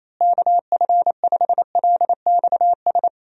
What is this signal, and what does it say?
Morse code, heard as a keyed tone, about 28 wpm, 700 Hz. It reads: KF5LXS
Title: USA Call Signs - 6 Characters - Repeated 2x - 28wpm